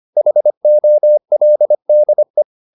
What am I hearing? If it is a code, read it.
Morse code, heard as a keyed tone, about 25 wpm, 600 Hz. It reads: HOLDE
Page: Ferie 25 wpm